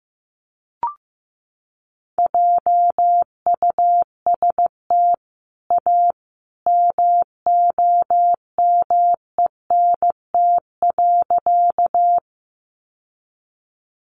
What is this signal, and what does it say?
Morse code, heard as a keyed tone, about 15 wpm, 700 Hz. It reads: JUST A MOMENT.
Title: Sentences from Top 1000 Words 15wpm